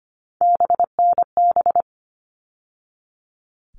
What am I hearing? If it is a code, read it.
Morse code, heard as a keyed tone, about 25 wpm, 700 Hz. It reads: BNB
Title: All Cut Numbers - 3-Digit 25wpm